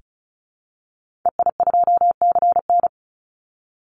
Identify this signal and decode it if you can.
Morse code, heard as a keyed tone, about 34 wpm, 700 Hz. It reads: EI2CN
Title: Call Signs Repeated 3x - 35wpm